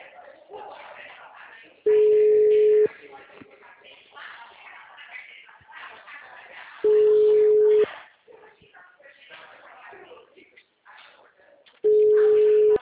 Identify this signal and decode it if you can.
Ringback tone (Europe, 425 Hz), with speech.
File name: out- -2004- - - .18366